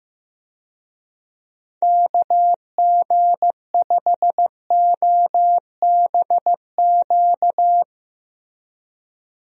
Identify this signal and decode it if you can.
Morse code, heard as a keyed tone, about 15 wpm, 700 Hz. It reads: KG5OBQ
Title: USA Call Signs - 6 Characters - Repeated 2x - 15wpm